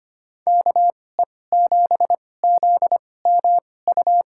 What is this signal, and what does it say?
Morse code, heard as a keyed tone, about 25 wpm, spaced as for 18 wpm, 700 Hz. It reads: KE7ZMU
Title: USA Call Signs - 2x Character Spacing - 25wpm